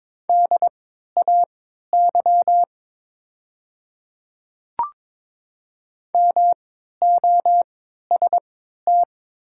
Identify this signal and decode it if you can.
Morse code, heard as a keyed tone, about 23 wpm, spaced as for 13 wpm, 700 Hz. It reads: DAY MOST
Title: 3x Character Spacing - Top 100 Words - 22wpm